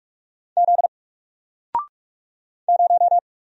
Morse code, heard as a keyed tone, about 46 wpm, 700 Hz. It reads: Z 0
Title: Single Letter-Number Alt 45wpm